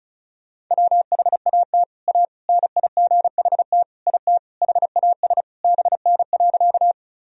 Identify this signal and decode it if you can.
Morse code, heard as a keyed tone, about 36 wpm, 700 Hz. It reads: WHAT A NIGHT IT HAS BN.
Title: Sentences from Top 300 Words and Abbreviations 35wpm